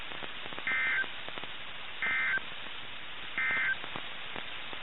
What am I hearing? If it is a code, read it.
Emergency Alert System (SAME) tones, heard as harsh buzzing bursts. It is heard from 0.6 to 3.8 s.
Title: NOAA Weather Radio - KIH61